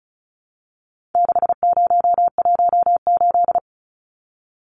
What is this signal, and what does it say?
Morse code, heard as a keyed tone, about 34 wpm, 700 Hz. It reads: 6018
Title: Numbers - 4-Digit 35wpm